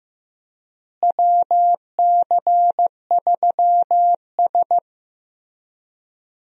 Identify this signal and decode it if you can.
Morse code, heard as a keyed tone, about 15 wpm, 700 Hz. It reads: WC3S
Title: USA Call Signs - 4 Characters - Repeated 3x - 15wpm